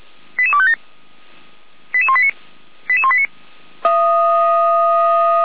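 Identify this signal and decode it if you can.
Five-tone selective call (ZVEI), heard as quick runs of five tones, tones of about 70 ms. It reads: 88257 88189 88189
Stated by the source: ZVEI five-tone selective call received by SDR, from the Feuerwehr Orlamünde audio archive.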